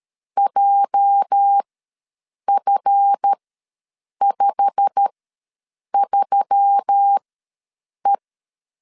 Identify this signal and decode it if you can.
Morse code, heard as a keyed tone, about 13 wpm, spaced as for 7 wpm, 800 Hz. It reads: JF53E
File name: LEZIONI / 8 YJ45 / 8 YJ45 ws 12wpm 3146